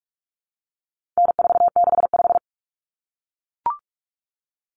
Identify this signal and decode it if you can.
Morse code, heard as a keyed tone, about 44 wpm, 700 Hz. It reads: N465
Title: Common Cut Numbers - 4-Digit 45wpm